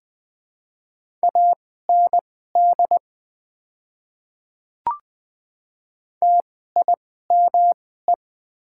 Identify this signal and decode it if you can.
Morse code, heard as a keyed tone, about 20 wpm, spaced as for 15 wpm, 700 Hz. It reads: AND TIME
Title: 2x Character Spacing - Top 100 Words - 20wpm